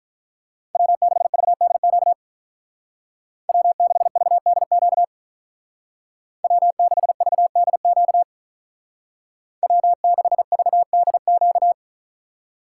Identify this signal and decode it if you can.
Morse code, heard as a keyed tone, about 44 wpm, 700 Hz. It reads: W6VDQ W6VDQ W6VDQ W6VDQ
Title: Speed Racing - USA 1x3 Call Signs 35wpm